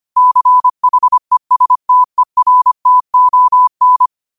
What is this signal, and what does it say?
Morse code, heard as a keyed tone, about 25 wpm, 1000 Hz. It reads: CHESTERTON